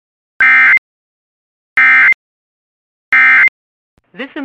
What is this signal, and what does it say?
Emergency Alert System (SAME) tones, heard as harsh buzzing bursts. It is heard from 0.4 to 3.5 s.